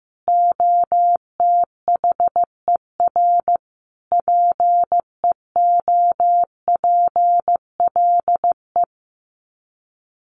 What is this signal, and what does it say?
Morse code, heard as a keyed tone, about 15 wpm, 700 Hz. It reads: OTHER PEOPLE
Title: Sets of 5 Words Encore2 15wpm